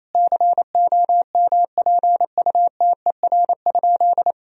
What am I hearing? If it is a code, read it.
Morse code, heard as a keyed tone, about 28 wpm, 700 Hz. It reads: COMPUTER?
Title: Sentences from Top 1000 Words - Review Part 3 - 28wpm